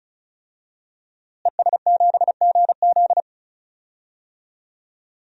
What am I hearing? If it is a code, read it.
Morse code, heard as a keyed tone, about 36 wpm, 700 Hz. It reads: ES7GZ
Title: DXCC-100 Call Signs - 5 Characters - Repeated 2x - 35wpm